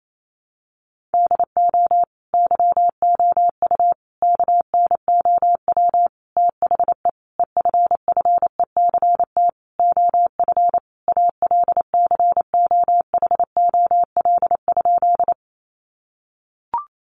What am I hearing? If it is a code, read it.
Morse code, heard as a keyed tone, about 28 wpm, 700 Hz. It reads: DO YOU KNOW THE EFFECT OF ALCOHOL?
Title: Sentences from Top 1400 Words 28wpm